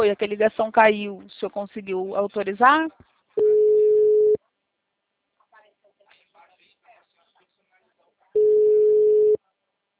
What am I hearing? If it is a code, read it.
Ringback tone (Europe, 425 Hz), with speech.